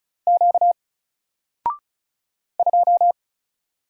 Morse code, heard as a keyed tone, about 36 wpm, 700 Hz. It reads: Q 2
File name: Single-Letter-Number-35wpm